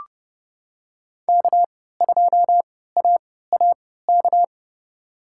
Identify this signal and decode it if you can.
Morse code, heard as a keyed tone, about 30 wpm, spaced as for 17 wpm, 700 Hz. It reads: K2AAK
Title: USA Call Signs - 5 Characters - 3x Character Spacing - 30wpm